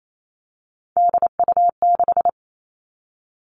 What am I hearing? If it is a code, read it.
Morse code, heard as a keyed tone, about 28 wpm, 700 Hz. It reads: DU6
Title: Three Challenging Letter-Number Combinations 28wpm